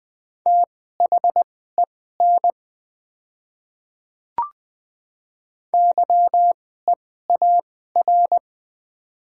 Morse code, heard as a keyed tone, about 20 wpm, spaced as for 14 wpm, 700 Hz. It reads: THEN YEAR
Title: 2x Character Spacing - Top 300 Words - 20wpm